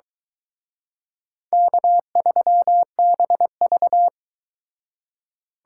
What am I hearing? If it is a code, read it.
Morse code, heard as a keyed tone, about 22 wpm, 700 Hz. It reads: K3BV